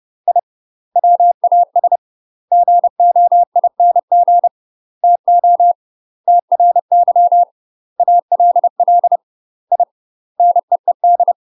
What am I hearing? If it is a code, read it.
Morse code, heard as a keyed tone, about 30 wpm, 700 Hz. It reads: I WAS GOING TO TRY ALL I NEED
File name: UK MASTER QSOs TRAINING / 30WPM / UK MASTER QSOs 0007